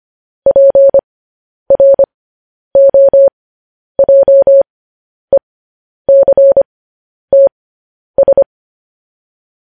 Morse code, heard as a keyed tone, about 25 wpm, spaced as for 10 wpm, 550 Hz. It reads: PROJECTS
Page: news headlines in morse code at 10 wpm